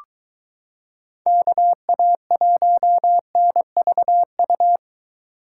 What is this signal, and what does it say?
Morse code, heard as a keyed tone, about 22 wpm, 700 Hz. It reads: KA1NVU